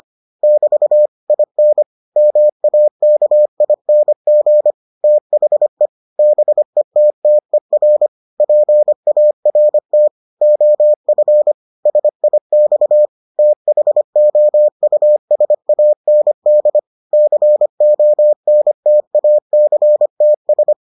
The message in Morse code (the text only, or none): <BT> IN MAKING THE BETTER PART OF SIX THOUSAND CONTACTS